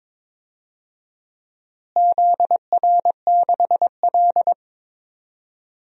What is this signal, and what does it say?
Morse code, heard as a keyed tone, about 22 wpm, 700 Hz. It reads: ZR6L
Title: DXCC-100 Call Signs - 4 Characters - Repeated 2x - 22wpm